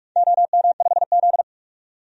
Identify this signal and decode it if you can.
Morse code, heard as a keyed tone, about 46 wpm, 700 Hz. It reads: OM5Z
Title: DXCC-100 Call Signs - 4 Characters - 45wpm